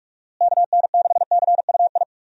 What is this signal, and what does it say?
Morse code, heard as a keyed tone, about 46 wpm, 700 Hz. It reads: KN6CUI
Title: USA Call Signs - 6 Characters - Repeated 3x - 45wpm